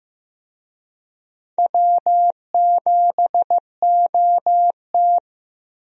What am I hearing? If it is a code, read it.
Morse code, heard as a keyed tone, about 15 wpm, 700 Hz. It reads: W7OT
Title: USA Call Signs - 4 Characters - Repeated 2x - 15wpm